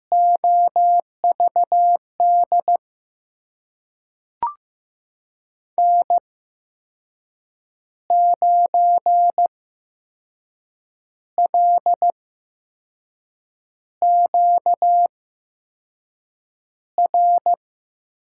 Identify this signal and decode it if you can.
Morse code, heard as a keyed tone, about 15 wpm, spaced as for 5 wpm, 700 Hz. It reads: OVD N9LQR
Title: USA Call Signs - 8x Character Spacing - 15wpm